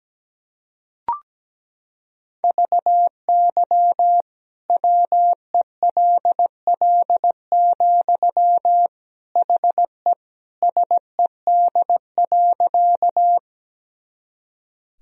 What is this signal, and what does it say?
Morse code, heard as a keyed tone, about 17 wpm, 700 Hz. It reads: VY WELL, HE SED.
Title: Sentences from Top 200 Words and Abbreviations 17wpm